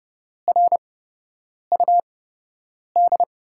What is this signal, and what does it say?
Morse code, heard as a keyed tone, about 30 wpm, spaced as for 8 wpm, 700 Hz. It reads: RUD
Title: USA Call Signs - 5 Characters - 8x Character Spacing - 30wpm